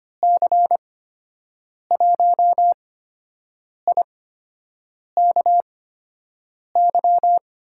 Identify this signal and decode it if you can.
Morse code, heard as a keyed tone, about 25 wpm, spaced as for 7 wpm, 700 Hz. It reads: C1IKY
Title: USA Call Signs - 6 Characters - 8x Character Spacing - 25wpm